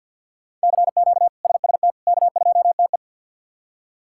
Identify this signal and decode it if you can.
Morse code, heard as a keyed tone, about 48 wpm, 700 Hz. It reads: CQ SST K2TE
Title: K1USN SST QSO Practice - 50wpm